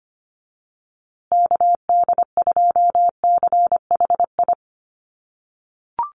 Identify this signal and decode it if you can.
Morse code, heard as a keyed tone, about 25 wpm, 700 Hz. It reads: KD2CHI
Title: USA Call Signs Repeated 2x 25wpm